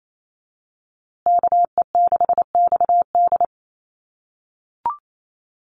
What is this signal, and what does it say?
Morse code, heard as a keyed tone, about 28 wpm, 700 Hz. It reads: KE6XD
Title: USA Call Signs Repeated 3x 28wpm